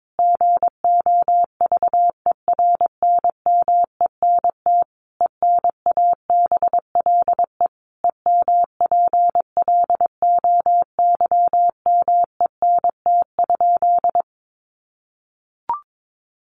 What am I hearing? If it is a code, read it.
Morse code, heard as a keyed tone, about 22 wpm, 700 Hz. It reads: GOVERNMENT ENABLE EMPLOYMENT?
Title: Sentences from Top 1500 Words 22wpm